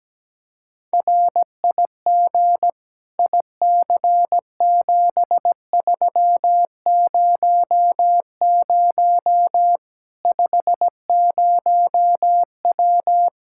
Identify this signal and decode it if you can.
Morse code, heard as a keyed tone, about 17 wpm, 700 Hz. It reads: RIG IC7300 50W
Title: Practice QSO - Exchange 2 - Info - 17wpm